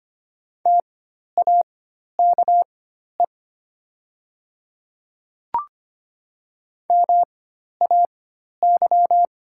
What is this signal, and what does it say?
Morse code, heard as a keyed tone, about 25 wpm, spaced as for 12 wpm, 700 Hz. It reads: TAKE MAY
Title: 4x Character Spacing - Top 200 Words - 25wpm